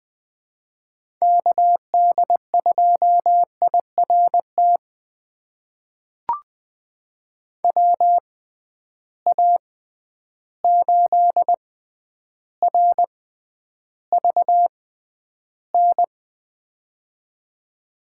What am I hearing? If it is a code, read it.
Morse code, heard as a keyed tone, about 20 wpm, spaced as for 10 wpm, 700 Hz. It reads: KD2IRT WA8RVN
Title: USA Call Signs - 6 Characters - 6x Character Spacing - 20wpm